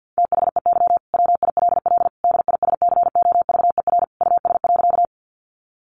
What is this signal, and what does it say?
Morse code, heard as a keyed tone, about 48 wpm, 700 Hz. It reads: THEY WILL DISCOVER US.
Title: Sentences from Top 1100 Words 50wpm